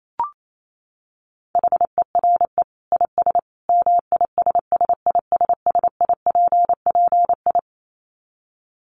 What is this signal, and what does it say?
Morse code, heard as a keyed tone, about 28 wpm, 700 Hz. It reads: HERE IS MISSISSIPPI
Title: Dit Challenge 28wpm